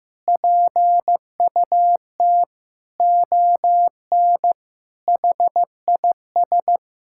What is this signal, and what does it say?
Morse code, heard as a keyed tone, about 15 wpm, 700 Hz. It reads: PUT ON HIS
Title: Sentences from Top 1600 Words 15wpm